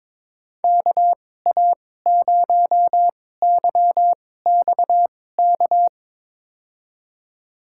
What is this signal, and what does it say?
Morse code, heard as a keyed tone, about 22 wpm, spaced as for 16 wpm, 700 Hz. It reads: KA0YXK
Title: USA Call Signs - 6 Characters - 2x Character Spacing - 22wpm